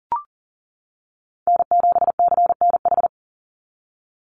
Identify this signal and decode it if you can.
Morse code, heard as a keyed tone, about 40 wpm, 700 Hz. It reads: N7CNH
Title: NA SOTA Chaser Callsigns 40wpm